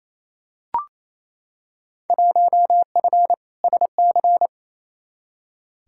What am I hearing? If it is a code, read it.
Morse code, heard as a keyed tone, about 28 wpm, 700 Hz. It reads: 1F SC
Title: Field Day Exchanges Encore - 28wpm